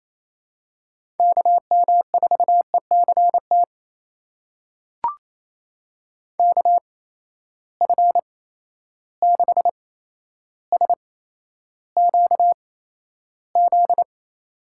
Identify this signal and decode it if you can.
Morse code, heard as a keyed tone, about 28 wpm, spaced as for 12 wpm, 700 Hz. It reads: KM4ECT KF6SQZ
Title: USA Call Signs - 6 Characters - 8x Character Spacing - 28wpm